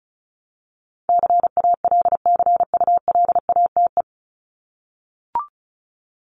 Morse code, heard as a keyed tone, about 34 wpm, 700 Hz. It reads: CALCULATE
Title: Top 1751-2000 Words 35wpm